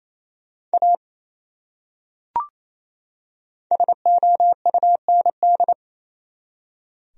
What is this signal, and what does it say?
Morse code, heard as a keyed tone, about 28 wpm, 700 Hz. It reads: A SOUND
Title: Top 100 Words 28wpm